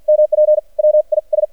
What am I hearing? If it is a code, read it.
Morse code, heard as a keyed tone, about 53 wpm, spaced as for 43 wpm, 610 Hz. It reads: MP WIS
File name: archived / CRSnet